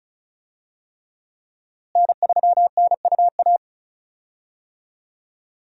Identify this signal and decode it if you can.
Morse code, heard as a keyed tone, about 36 wpm, 700 Hz. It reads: N3NUA
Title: USA Call Signs - 5 Characters - Repeated 3x - 35wpm